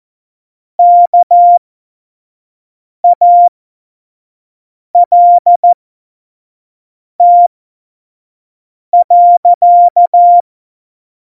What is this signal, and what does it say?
Morse code, heard as a keyed tone, about 14 wpm, spaced as for 5 wpm, 700 Hz. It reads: KALT.
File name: Morsekurs / Witze / Witz 13. August 2019, 25 F70